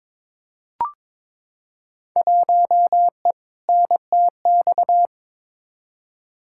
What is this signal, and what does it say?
Morse code, heard as a keyed tone, about 21 wpm, 700 Hz. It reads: 1E NTX
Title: Field Day Exchanges - 22wpm